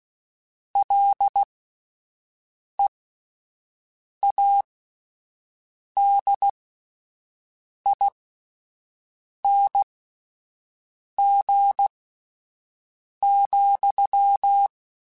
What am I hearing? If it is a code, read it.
Morse code, heard as a keyed tone, about 16 wpm, spaced as for 5 wpm, 800 Hz. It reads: LEADING,